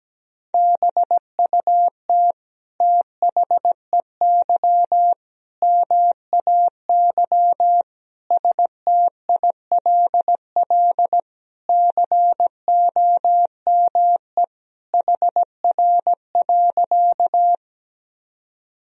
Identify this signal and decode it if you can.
Morse code, heard as a keyed tone, about 17 wpm, 700 Hz. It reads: BUT THEY MAY STILL COME HR.
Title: Sentences from Top 300 Words and Abbreviations 17wpm